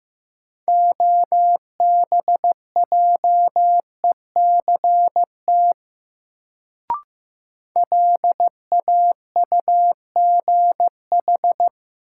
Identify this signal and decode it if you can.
Morse code, heard as a keyed tone, about 15 wpm, 700 Hz. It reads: OBJECT LAUGH
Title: Sentences from Top 500 Words - New Word Drill - 15wpm